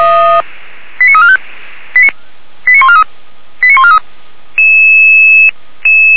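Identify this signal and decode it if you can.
Five-tone selective call (ZVEI), heard as quick runs of five tones, tones of about 70 ms. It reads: 89345 89143 89143
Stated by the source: Received on an SDR.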